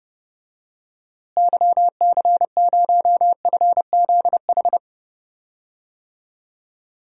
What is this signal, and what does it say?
Morse code, heard as a keyed tone, about 30 wpm, 700 Hz. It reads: YC0FZH
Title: DXCC-100 Call Signs - 6 Characters - Repeated 2x - 30wpm